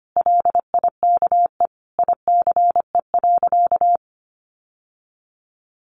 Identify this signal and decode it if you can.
Morse code, heard as a keyed tone, about 25 wpm, 700 Hz. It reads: LIKE ICE.
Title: Sentences from Top 1000 Words 25wpm